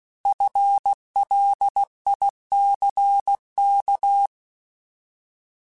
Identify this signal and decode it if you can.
Morse code, heard as a keyed tone, about 16 wpm, 800 Hz. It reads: FLICK
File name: Alphabetical / Alphabetical mp3s / Fs @15